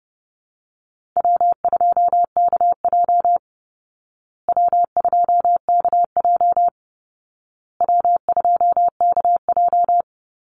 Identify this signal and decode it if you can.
Morse code, heard as a keyed tone, about 30 wpm, 700 Hz. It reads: W2KJ W2KJ W2KJ
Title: Easy Call Signs Repeated 3x - 30wpm